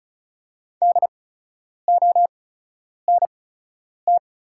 Morse code, heard as a keyed tone, about 36 wpm, spaced as for 10 wpm, 700 Hz. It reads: DONT
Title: 8x Character Spacing - Top 500 Words - 35wpm